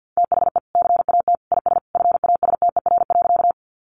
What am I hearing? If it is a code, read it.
Morse code, heard as a keyed tone, about 50 wpm, 700 Hz. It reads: THE CAT IS FASTER.
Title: Additional Easy Phrases 50wpm